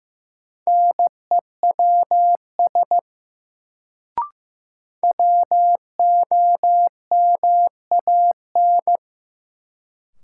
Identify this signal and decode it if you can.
Morse code, heard as a keyed tone, about 15 wpm, 700 Hz. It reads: NEWS WOMAN
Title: Sets of 5 Words - New Word Drill - 15wpm